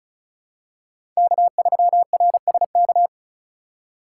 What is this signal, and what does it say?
Morse code, heard as a keyed tone, about 36 wpm, 700 Hz. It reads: K3RSK
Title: USA Call Signs - 5 Characters - 3x Character Spacing - 35wpm